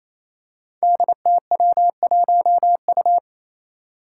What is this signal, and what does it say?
Morse code, heard as a keyed tone, about 29 wpm, 700 Hz. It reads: DTW1U
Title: Five Challenging Letter-Number Combinations 28wpm